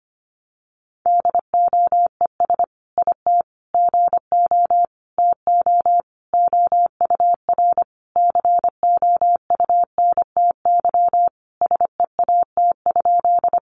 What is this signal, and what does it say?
Morse code, heard as a keyed tone, about 25 wpm, 700 Hz. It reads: DOES IT GO TO OUR COUNTY SEAT?
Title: Sentences from Top 1500 Words 25wpm